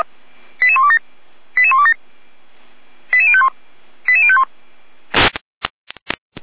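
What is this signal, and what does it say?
Five-tone selective call (ZVEI), heard as quick runs of five tones, tones of about 70 ms. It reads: 88127 88127 88051 88051
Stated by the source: ZVEI five-tone selective call received on SDR.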